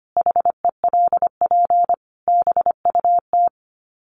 Morse code, heard as a keyed tone, about 25 wpm, 700 Hz. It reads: HELP BUT
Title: Sets of 5 Words Encore2 25wpm